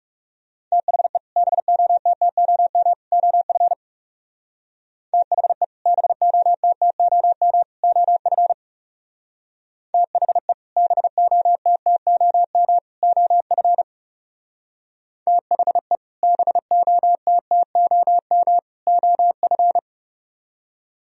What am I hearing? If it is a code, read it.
Morse code, heard as a keyed tone, about 36 wpm, 700 Hz. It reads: THE BOTTOM OF THE BOTTOM OF THE BOTTOM OF THE BOTTOM OF
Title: Speed-Racing - Sets of 3 Words 30wpm